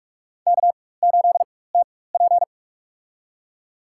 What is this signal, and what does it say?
Morse code, heard as a keyed tone, about 48 wpm, spaced as for 21 wpm, 700 Hz. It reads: K8TP
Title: USA Call Signs - 4x Character Spacing - 45wpm